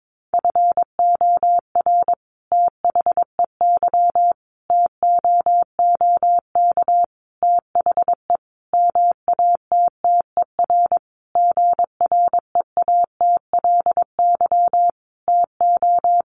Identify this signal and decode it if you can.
Morse code, heard as a keyed tone, about 22 wpm, 700 Hz. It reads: FOR THEY TOOK THE MATTER GREATLY TO
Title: Sentences from Top 1400 Words 22wpm